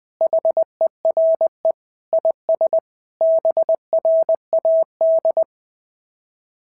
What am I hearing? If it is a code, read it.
Morse code, heard as a keyed tone, about 20 wpm, 690 Hz. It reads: HERE IS BRAD
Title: SOTA QSO Practice 20wpm